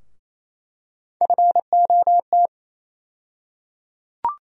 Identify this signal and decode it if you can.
Morse code, heard as a keyed tone, about 28 wpm, 700 Hz. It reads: FOT
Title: USA Call Sign Postfixes - 3 Letters 28wpm